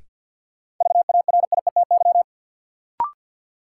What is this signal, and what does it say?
Morse code, heard as a keyed tone, about 49 wpm, 700 Hz. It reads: VARIETY
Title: Sets of 3 Words - New Word Drill - 50wpm